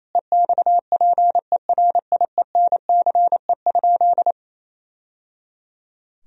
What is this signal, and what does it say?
Morse code, heard as a keyed tone, about 28 wpm, 700 Hz. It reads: EXPERIENCE?
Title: Sentences from Top 1000 Words - Review Part 3 - 28wpm